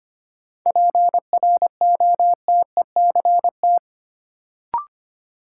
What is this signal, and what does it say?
Morse code, heard as a keyed tone, about 25 wpm, 700 Hz. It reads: PROTECT
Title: Sentences from Top 900 Words - New Word Drill - 25wpm